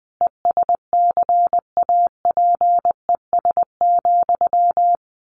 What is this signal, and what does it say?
Morse code, heard as a keyed tone, about 20 wpm, 700 Hz. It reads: ESCAPES,
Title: A Sound Of Thunder 20wpm